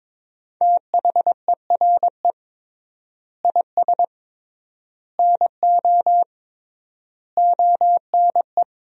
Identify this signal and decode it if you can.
Morse code, heard as a keyed tone, about 21 wpm, 700 Hz. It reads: THERE IS NO ONE
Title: 3x Word-Spacing - Sentences from Top 500 Words - 22wpm